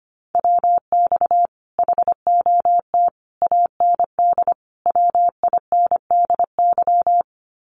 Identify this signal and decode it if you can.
Morse code, heard as a keyed tone, about 25 wpm, 700 Hz. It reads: WX HOT AND WINDY
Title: Common QSO Phrases Encore 25wpm